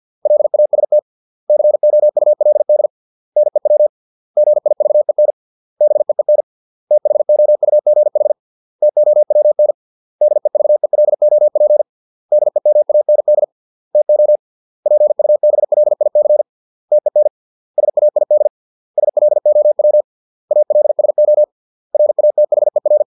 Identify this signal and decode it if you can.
Morse code, heard as a keyed tone, about 50 wpm, 600 Hz. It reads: LAST COULD NEW GIVEN BEEN THOUGH TOWN DEVELOP DEMAND TO PUBLIC TEN SAID SLOW ALSO RATHER